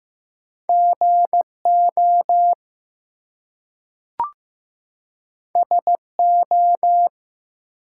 Morse code, heard as a keyed tone, about 15 wpm, 700 Hz. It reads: GO SO
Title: Two Letter Words - 15wpm